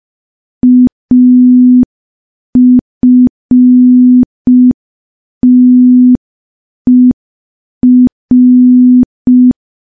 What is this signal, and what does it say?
Morse code, heard as a keyed tone, about 5 wpm, 250 Hz. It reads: AFTER
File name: morse / morse-training-05WPM